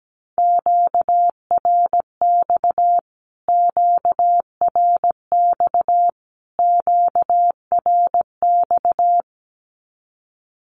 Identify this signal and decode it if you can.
Morse code, heard as a keyed tone, about 17 wpm, 700 Hz. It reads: QRX QRX QRX
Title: Q-Codes Repeated 3x 17wpm